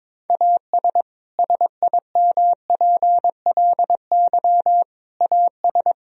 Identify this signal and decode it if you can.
Morse code, heard as a keyed tone, about 22 wpm, 700 Hz. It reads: AS SIMPLY AS
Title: Sentences from Top 1000 Words - Review Part 4 - 22wpm